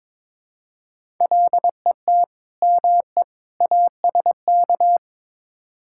Morse code, heard as a keyed tone, about 22 wpm, 700 Hz. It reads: LET ME ASK
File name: practice / 22wpm / Build A Phrase - 5-Word Phrases - 22wpm